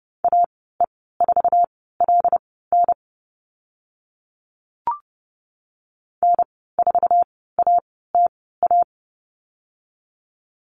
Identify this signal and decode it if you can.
Morse code, heard as a keyed tone, about 30 wpm, spaced as for 17 wpm, 700 Hz. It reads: AE4LN N4ATA
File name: Lesson 15g - Number 5 - Call Signs 30wpm